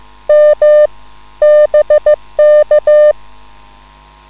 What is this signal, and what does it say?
Morse code, heard as a keyed tone, about 15 wpm, 600 Hz. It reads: M BK